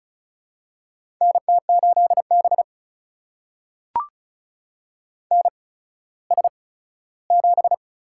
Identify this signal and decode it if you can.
Morse code, heard as a keyed tone, about 36 wpm, spaced as for 17 wpm, 700 Hz. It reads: NT8B NS7
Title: USA Call Signs - 4 Characters - 8x Character Spacing - 35wpm